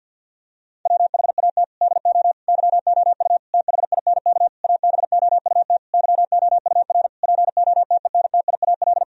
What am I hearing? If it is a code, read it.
Morse code, heard as a keyed tone, about 50 wpm, 700 Hz. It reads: WHAT DO YOU THINK ABOUT YOUR POTENTIAL
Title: Sentences from Top 1000 Words - Review Part 4 - 50wpm